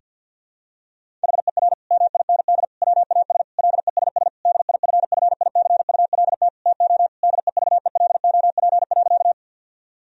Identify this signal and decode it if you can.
Morse code, heard as a keyed tone, about 49 wpm, 700 Hz. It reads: HER MIND WAS LESS DIFFICULT TO DEVELOP.
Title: Sentences from Top 1000 Words - Review Part 2 - 50wpm